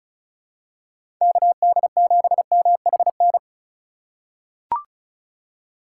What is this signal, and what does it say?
Morse code, heard as a keyed tone, about 36 wpm, 700 Hz. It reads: KD7MHN
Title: USA Call Signs - 2x Character Spacing - 35wpm